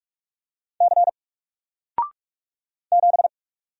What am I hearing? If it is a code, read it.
Morse code, heard as a keyed tone, about 45 wpm, 700 Hz. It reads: C 7
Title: Single Character (Letter:Number:Punctuation:Prosign) 45wpm